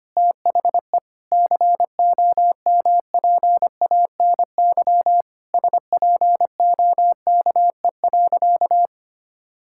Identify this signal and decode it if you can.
Morse code, heard as a keyed tone, about 25 wpm, 700 Hz. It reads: THE COMPANY SPOKE.
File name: practice / 25wpm / Sentences with 3-Word Phrases - 25wpm